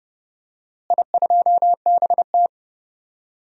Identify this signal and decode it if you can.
Morse code, heard as a keyed tone, about 30 wpm, 700 Hz. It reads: I2BT
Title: DXCC-100 Call Signs - 4 Characters - Repeated 2x - 30wpm